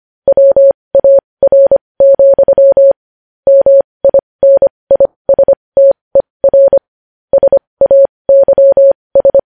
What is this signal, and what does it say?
Morse code, heard as a keyed tone, about 25 wpm, spaced as for 20 wpm, 550 Hz. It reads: WAR, MINISTER SAYS
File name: podcast / News headlines-1x-25x20